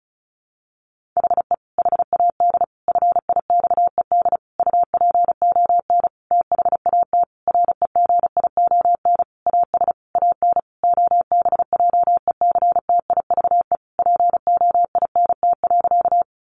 Morse code, heard as a keyed tone, about 34 wpm, 700 Hz. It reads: HE HAD FIXED UPON THAT REGION AS AN OBJECTIVE POINT.
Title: Sentences from Top 1500 Words 35wpm